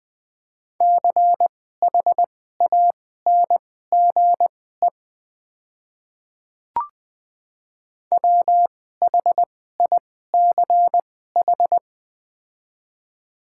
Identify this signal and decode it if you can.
Morse code, heard as a keyed tone, about 20 wpm, spaced as for 14 wpm, 690 Hz. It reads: CHANGE WHICH
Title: 2x Character Spacing - Top 500 Words - 20wpm